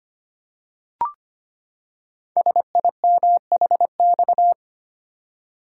Morse code, heard as a keyed tone, about 25 wpm, 700 Hz. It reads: SIMHX
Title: Code Groups - 5 Letters - 25wpm